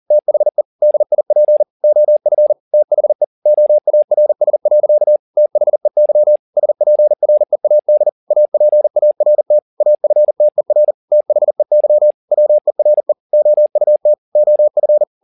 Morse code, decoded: THE DIP OF THE OARS. THEY SPREAD APART AFTER THEY WERE OUT OF

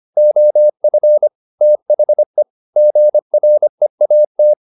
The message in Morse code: OF THE GREAT

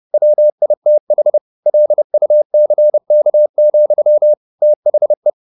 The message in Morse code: WITH LUCK, THE

